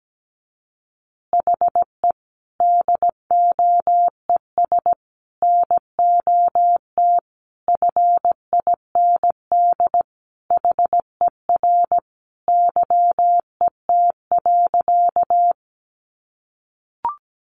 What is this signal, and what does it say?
Morse code, heard as a keyed tone, about 17 wpm, 700 Hz. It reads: HE DOES NOT FIND HER YET.
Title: Easy Phrases 17wpm